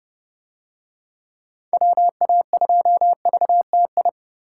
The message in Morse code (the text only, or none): WA2VTI